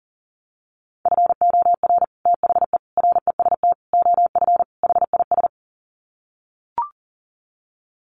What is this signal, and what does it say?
Morse code, heard as a keyed tone, about 40 wpm, 700 Hz. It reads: FOR THE REST OF HIS
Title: Sets of 5 Words 40wpm